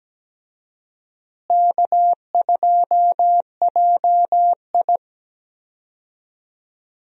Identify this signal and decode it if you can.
Morse code, heard as a keyed tone, about 17 wpm, 700 Hz. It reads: K2JI
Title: USA Call Signs - 4 Characters - Repeated 3x - 17wpm